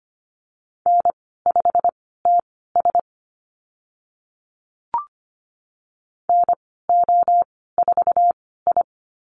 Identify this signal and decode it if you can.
Morse code, heard as a keyed tone, about 25 wpm, spaced as for 16 wpm, 700 Hz. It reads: N5TS NO4I
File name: Lesson 15g - Number 5 - Call Signs 25wpm